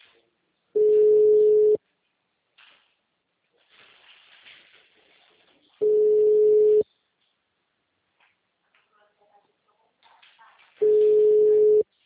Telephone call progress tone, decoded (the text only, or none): ringback tone